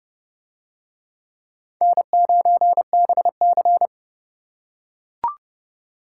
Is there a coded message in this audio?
Yes: Morse code from 1.8 to 3.9 s, heard as a keyed tone, about 31 wpm, 700 Hz. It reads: N9BC